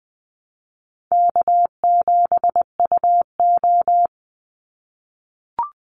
Morse code, heard as a keyed tone, about 20 wpm, 700 Hz. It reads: K7UO
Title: USA Call Signs 20wpm